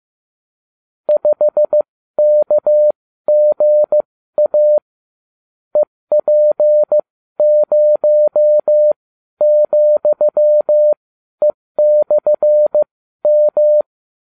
Morse code, heard as a keyed tone, about 15 wpm, spaced as for 12 wpm, 600 Hz. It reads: 5KGA EP0,E/M